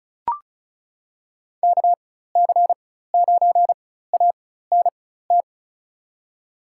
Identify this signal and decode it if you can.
Morse code, heard as a keyed tone, about 36 wpm, spaced as for 16 wpm, 700 Hz. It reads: KC9ANT
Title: USA Call Signs - 4x Character Spacing - 35wpm